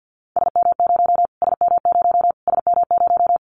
Morse code, heard as a keyed tone, about 49 wpm, 700 Hz. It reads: SM0 SM0 SM0
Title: Callsign Prefixes for Top 25 DXCC - Repeated 3x - 50wpm